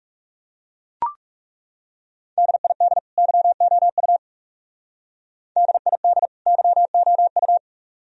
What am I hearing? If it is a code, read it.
Morse code, heard as a keyed tone, about 37 wpm, 700 Hz. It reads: DID YOU DID YOU
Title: Speed-Racing - Sets of 2 Words 30wpm